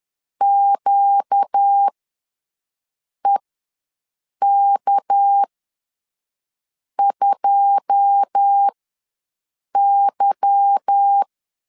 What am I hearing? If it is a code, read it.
Morse code, heard as a keyed tone, about 10 wpm, spaced as for 5 wpm, 800 Hz. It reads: QEK2Y
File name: LEZIONI / 9 2 ws 10wpm 4130